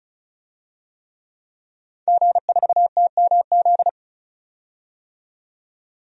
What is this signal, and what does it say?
Morse code, heard as a keyed tone, about 34 wpm, 700 Hz. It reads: G4TMZ
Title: DXCC-100 Call Signs - 5 Characters - 35wpm